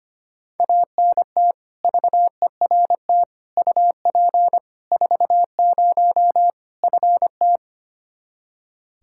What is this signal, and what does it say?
Morse code, heard as a keyed tone, about 25 wpm, 700 Hz. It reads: ANT VERT UP 40 FT
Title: Practice QSO - Exchange 2 - Info - 25wpm